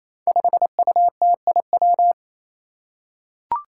Morse code, heard as a keyed tone, about 28 wpm, 700 Hz. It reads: 5UTIW
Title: Five Challenging Letter-Number Combinations 28wpm